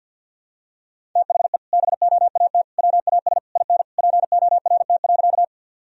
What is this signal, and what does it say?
Morse code, heard as a keyed tone, about 52 wpm, 700 Hz. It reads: THE BOAT WAS IN PORT.
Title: Sentences from Top 1000 Words - Review Part 1 - 50wpm